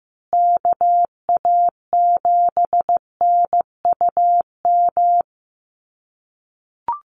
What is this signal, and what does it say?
Morse code, heard as a keyed tone, about 15 wpm, 700 Hz. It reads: KA7NUM
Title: USA Call Signs Repeated 2x 15wpm